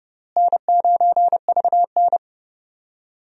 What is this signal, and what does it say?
Morse code, heard as a keyed tone, about 31 wpm, 700 Hz. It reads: N9VN
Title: USA Call Signs - 4 Characters - Repeated 3x - 30wpm